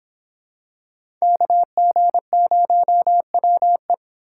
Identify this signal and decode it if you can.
Morse code, heard as a keyed tone, about 26 wpm, 700 Hz. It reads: KG0WE